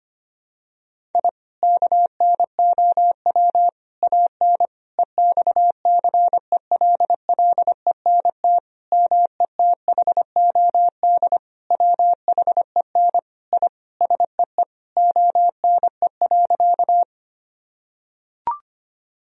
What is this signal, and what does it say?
Morse code, heard as a keyed tone, about 25 wpm, 700 Hz. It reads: I KNOW AN EXCELLENT METHOD WHEN I SEE ONE.
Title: Sentences from Top 1000 Words - Review Part 3 - 25wpm